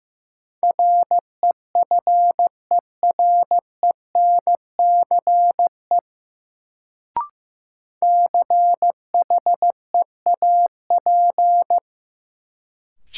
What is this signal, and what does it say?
Morse code, heard as a keyed tone, about 15 wpm, 700 Hz. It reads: REFERENCE CHEAP
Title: Sentences from Top 1200 Words - New Word Drill Alt - 15wpm